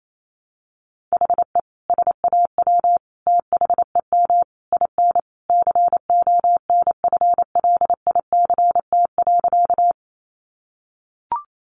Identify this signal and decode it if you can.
Morse code, heard as a keyed tone, about 28 wpm, 700 Hz. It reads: HE SAW THEM IN CONFLICT.
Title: Sentences from Top 1300 Words 28wpm